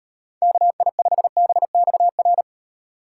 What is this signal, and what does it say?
Morse code, heard as a keyed tone, about 38 wpm, 700 Hz. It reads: KI5BXR